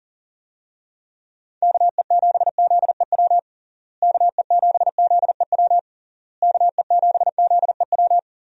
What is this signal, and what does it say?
Morse code, heard as a keyed tone, about 40 wpm, 700 Hz. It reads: KE7ZEW KE7ZEW KE7ZEW
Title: USA Call Signs - 6 Characters - Repeated 3x - 40wpm